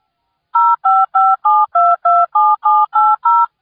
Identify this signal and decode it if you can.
DTMF, heard as telephone touch tones, tones of about 220 ms. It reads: 0557227780